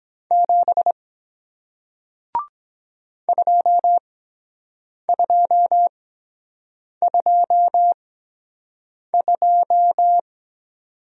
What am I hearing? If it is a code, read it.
Morse code, heard as a keyed tone, about 23 wpm, spaced as for 7 wpm, 700 Hz. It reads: 7 2222